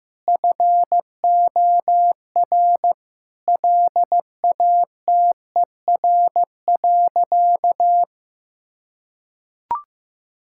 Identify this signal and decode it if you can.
Morse code, heard as a keyed tone, about 14 wpm, 700 Hz. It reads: FOR LATER.
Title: Sentences from Top 1000 Words - Review Part 6 - 15wpm